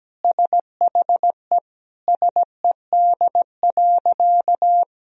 Morse code, heard as a keyed tone, about 17 wpm, 700 Hz. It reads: SHE SED.
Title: Sentences from Top 100 Words and Abbreviations 17wpm